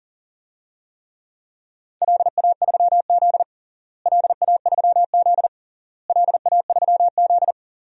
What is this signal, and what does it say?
Morse code, heard as a keyed tone, about 40 wpm, 700 Hz. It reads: LA3Z LA3Z LA3Z
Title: DXCC-100 Call Signs - 4 Characters - Repeated 3x - 40wpm